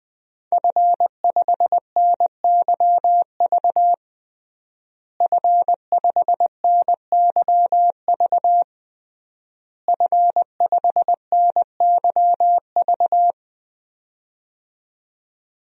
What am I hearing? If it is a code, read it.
Morse code, heard as a keyed tone, about 20 wpm, 700 Hz. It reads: F5NYV F5NYV F5NYV
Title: DXCC-100 Call Signs - 5 Characters - Repeated 3x - 20wpm